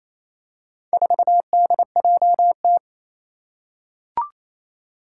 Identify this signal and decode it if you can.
Morse code, heard as a keyed tone, about 28 wpm, 700 Hz. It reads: 4DJT
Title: Four Challenging Letter-Number Combinations 28wpm